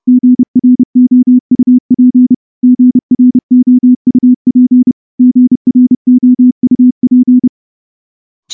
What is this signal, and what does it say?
Morse code, heard as a keyed tone, about 29 wpm, 250 Hz. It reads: GROUP GROUP GROUP
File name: morse-training-30WPM